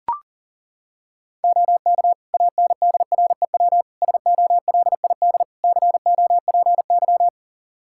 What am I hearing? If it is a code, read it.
Morse code, heard as a keyed tone, about 40 wpm, 700 Hz. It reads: OK ANDREW SOLID COPY
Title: Practice QSO - Exchange 3 - More Info - 40wpm